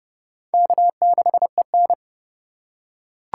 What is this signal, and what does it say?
Morse code, heard as a keyed tone, about 29 wpm, 690 Hz. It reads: K6EN